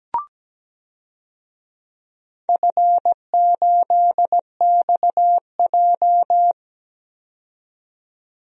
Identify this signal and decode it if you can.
Morse code, heard as a keyed tone, about 17 wpm, 700 Hz. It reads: F8XJ
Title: DXCC-100 Call Signs - 4 Characters - 17wpm